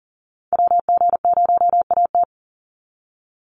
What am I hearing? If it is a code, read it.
Morse code, heard as a keyed tone, about 40 wpm, 700 Hz. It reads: WG0AT
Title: NA SOTA Chaser Callsigns 40wpm